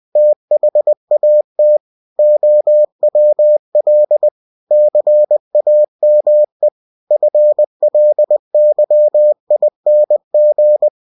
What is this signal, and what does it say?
Morse code, heard as a keyed tone, about 20 wpm, 600 Hz. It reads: THAT OWL CAME FLYING
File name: cw-books / winnie-the-pooh / winnie-20wpm-page-0096